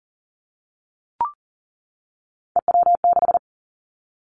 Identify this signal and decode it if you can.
Morse code, heard as a keyed tone, about 40 wpm, 700 Hz. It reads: EW6
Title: Callsign Prefixes for Top 50 DXCC - 40wpm